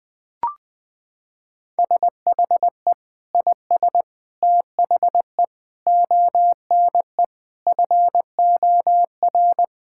Morse code, heard as a keyed tone, about 20 wpm, 700 Hz. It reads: SHE IS THE ONE FOR
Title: Sentences with 2-Word Phrases - 20wpm